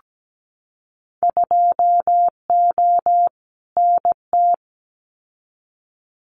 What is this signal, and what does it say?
Morse code, heard as a keyed tone, about 17 wpm, 700 Hz. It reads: 2O NT
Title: Winter Field Day Exchanges Encore 17wpm